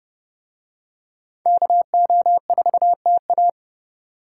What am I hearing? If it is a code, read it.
Morse code, heard as a keyed tone, about 28 wpm, 700 Hz. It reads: KO4TA